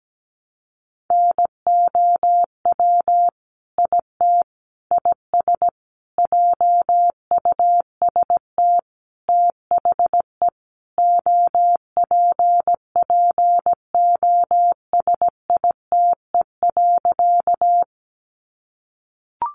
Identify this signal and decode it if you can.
Morse code, heard as a keyed tone, about 17 wpm, 700 Hz. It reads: NOW IT IS JUST THE OPPOSITE.
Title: Sentences from Top 1300 Words 17wpm